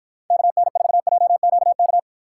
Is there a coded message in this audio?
Yes: Morse code from 0.3 to 2.0 s, heard as a keyed tone, about 53 wpm, 700 Hz. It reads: KN4JQK